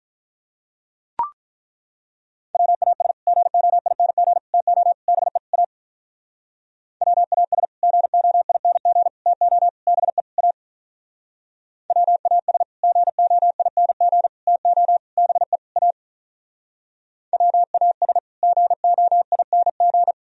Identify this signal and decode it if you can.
Morse code, heard as a keyed tone, about 42 wpm, 700 Hz. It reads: WAS GOING TO BE A WAS GOING TO BE A WAS GOING TO BE A WAS GOING
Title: Speed-Racing - Sets of 5 Words 35wpm